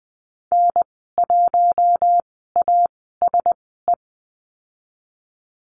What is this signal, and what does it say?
Morse code, heard as a keyed tone, about 20 wpm, spaced as for 15 wpm, 700 Hz. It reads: N1ASE
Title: 048 Lesson 9g - Number 4 - Call Signs 20wpm